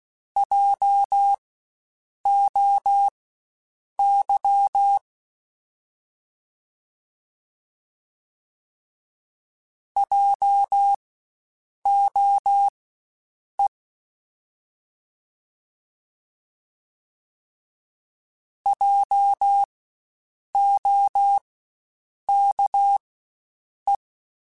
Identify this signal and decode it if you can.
Morse code, heard as a keyed tone, about 16 wpm, spaced as for 7 wpm, 800 Hz. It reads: JOY JOE JOKE
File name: Spaced Alphas / Spaced Alphas mp3s / Js sp @15